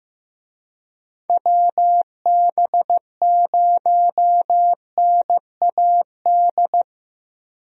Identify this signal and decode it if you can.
Morse code, heard as a keyed tone, about 15 wpm, 700 Hz. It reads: WB0NAD
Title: USA Call Signs - 6 Characters - 15wpm